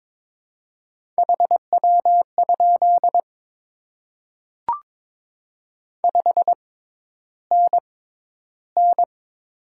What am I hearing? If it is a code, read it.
Morse code, heard as a keyed tone, about 23 wpm, spaced as for 11 wpm, 700 Hz. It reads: HW? 5NN
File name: Top 20 QSO Elements - 6x Character Spacing - 22wpm